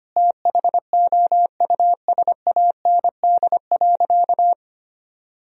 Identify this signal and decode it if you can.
Morse code, heard as a keyed tone, about 25 wpm, 700 Hz. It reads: THOUSAND.
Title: Sentences from Top 1000 Words - Review Part 3 - 25wpm